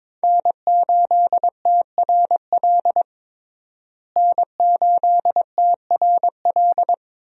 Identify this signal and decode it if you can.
Morse code, heard as a keyed tone, about 22 wpm, 700 Hz. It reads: N8TRL N8TRL
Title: USA Call Signs - 5 Characters - Repeated 2x - 22wpm